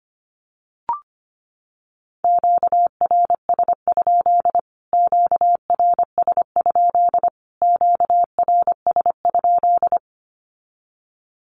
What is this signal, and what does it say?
Morse code, heard as a keyed tone, about 25 wpm, 700 Hz. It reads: QRS? QRS? QRS?
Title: Q-Codes Repeated 3x 25wpm